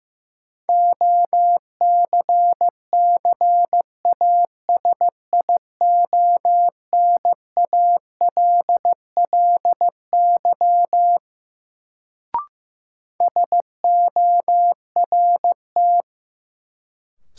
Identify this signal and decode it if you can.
Morse code, heard as a keyed tone, about 15 wpm, 700 Hz. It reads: OCCASIONALLY SORT
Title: Sentences from Top 1300 Words - New Word Drill Alt - 15wpm